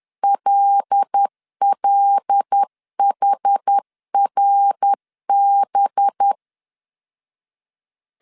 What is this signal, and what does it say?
Morse code, heard as a keyed tone, about 10 wpm, 800 Hz. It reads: LLHRB
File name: CW / DET / 1 6 1500C S B 10wpm 4137